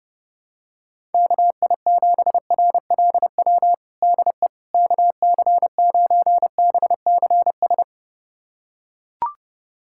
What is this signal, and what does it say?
Morse code, heard as a keyed tone, about 30 wpm, 700 Hz. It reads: KI7RLW DE KC9BCS
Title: USA Callsign DE Callsign - 6 Characters - 30wpm